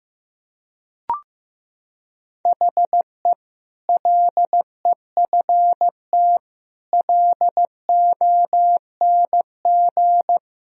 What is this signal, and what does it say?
Morse code, heard as a keyed tone, about 15 wpm, 700 Hz. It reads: HE LEFT LONG
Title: Sentences from Top 1000 Words - Review Part 3 - 15wpm